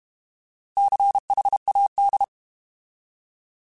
Morse code, heard as a keyed tone, about 32 wpm, 800 Hz. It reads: CHAD
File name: Alphabetical / Alphabetical mp3s / Cs @30